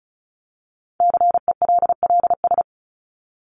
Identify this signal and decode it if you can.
Morse code, heard as a keyed tone, about 34 wpm, 700 Hz. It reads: CELLS
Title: ES:S Suffix - Top 100 Words with Base Words 35wpm